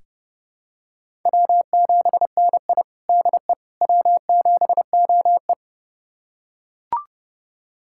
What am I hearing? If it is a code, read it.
Morse code, heard as a keyed tone, about 30 wpm, 700 Hz. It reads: W7NI DE W7OE
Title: USA Callsign DE Callsign - 4 Characters - 30wpm